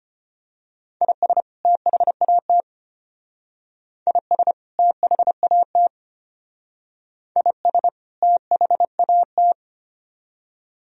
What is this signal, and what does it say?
Morse code, heard as a keyed tone, about 31 wpm, 700 Hz. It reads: IS THAT IS THAT IS THAT
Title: Speed-Racing - Sets of 2 Words 25wpm